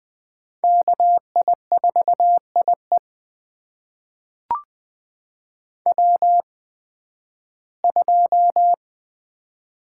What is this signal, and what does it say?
Morse code, heard as a keyed tone, about 20 wpm, spaced as for 13 wpm, 700 Hz. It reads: KI4IE W2
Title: USA Call Signs - 5 Characters - 8x Character Spacing - 20wpm